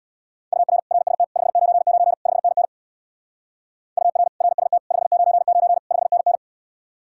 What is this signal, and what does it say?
Morse code, heard as a keyed tone, about 65 wpm, 700 Hz. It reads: UR RST 599 5NN UR RST 599 5NN